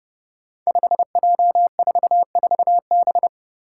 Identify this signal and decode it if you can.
Morse code, heard as a keyed tone, about 32 wpm, 700 Hz. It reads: 5J44B
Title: Five Challenging Letter-Number Combinations 30wpm